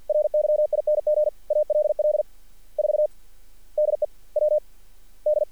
Morse code, heard as a keyed tone, about 46 wpm, 600 Hz. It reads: RYING ALL 4 DE W D
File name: archived / CRSnet